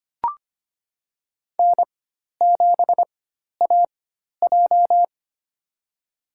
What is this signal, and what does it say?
Morse code, heard as a keyed tone, about 25 wpm, spaced as for 12 wpm, 700 Hz. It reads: N7AJ
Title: USA Call Signs - 4 Characters - 4x Character Spacing - 25wpm